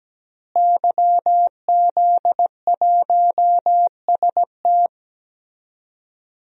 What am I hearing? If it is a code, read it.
Morse code, heard as a keyed tone, about 17 wpm, 700 Hz. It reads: YZ1ST
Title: DXCC-100 Call Signs - 5 Characters - Repeated 3x - 17wpm